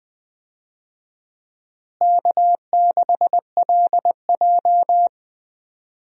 Morse code, heard as a keyed tone, about 20 wpm, 700 Hz. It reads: K6LJ
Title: USA Call Signs - 4 Characters - Repeated 3x - 20wpm